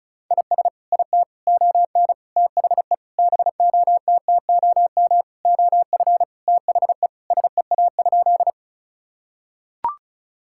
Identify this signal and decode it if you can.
Morse code, heard as a keyed tone, about 36 wpm, 700 Hz. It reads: IS IT ON THE BOTTOM OF THE SEA?
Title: Sentences from Top 1000 Words - Review Part 6 - 35wpm